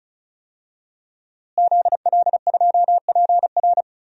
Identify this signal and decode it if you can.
Morse code, heard as a keyed tone, about 36 wpm, 700 Hz. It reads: ZL2PR
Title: DXCC-100 Call Signs - 5 Characters - Repeated 3x - 35wpm